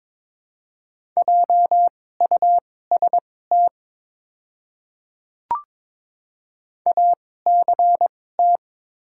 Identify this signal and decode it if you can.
Morse code, heard as a keyed tone, about 21 wpm, spaced as for 16 wpm, 700 Hz. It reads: JUST ACT
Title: 2x Character Spacing - Top 200 Words - 22wpm